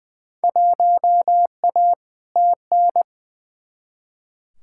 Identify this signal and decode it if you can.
Morse code, heard as a keyed tone, about 20 wpm, 700 Hz. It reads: 1A TN
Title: Field Day Exchanges - 20wpm